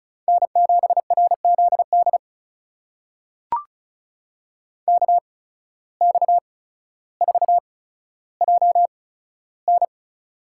Morse code, heard as a keyed tone, about 36 wpm, spaced as for 15 wpm, 700 Hz. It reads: N7RZD KX4JN
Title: USA Call Signs - 5 Characters - 8x Character Spacing - 35wpm